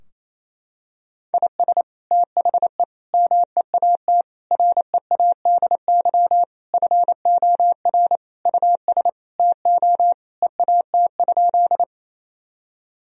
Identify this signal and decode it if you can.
Morse code, heard as a keyed tone, about 28 wpm, 700 Hz. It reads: IS THE MEAT READY FOR US TO EAT?
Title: Sentences from Top 1000 Words - Review Part 3 - 28wpm